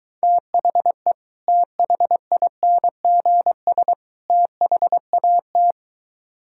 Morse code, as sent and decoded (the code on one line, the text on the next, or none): - .... / - .... .. -. --. ... / - .... .- -
THE THINGS THAT